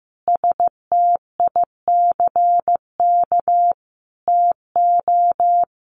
Morse code, heard as a keyed tone, about 15 wpm, 700 Hz. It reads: STICK TO
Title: Sentences from Top 1200 Words 15wpm